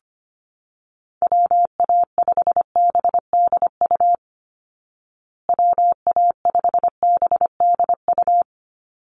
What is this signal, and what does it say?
Morse code, heard as a keyed tone, about 25 wpm, 700 Hz. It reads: WA5BDU WA5BDU
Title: Call Signs Repeated 2x - 25wpm